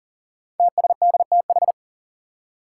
Morse code, heard as a keyed tone, about 40 wpm, 700 Hz. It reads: TSDTH
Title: Five Challenging Letter-Number Combinations 40wpm